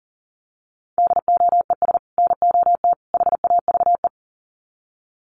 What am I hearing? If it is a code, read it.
Morse code, heard as a keyed tone, about 40 wpm, 700 Hz. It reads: DOES NOT HAVE
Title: Sets of 3 Words 40wpm